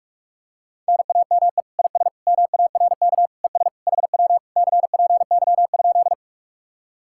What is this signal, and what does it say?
Morse code, heard as a keyed tone, about 49 wpm, 700 Hz. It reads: NAME IS MARK ES HW CPY?